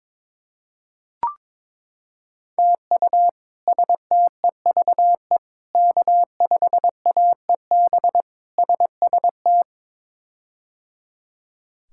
Morse code, heard as a keyed tone, about 22 wpm, 700 Hz. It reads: TU STEVE K5AEB SST
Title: K1USN SST QSO Practice - 22wpm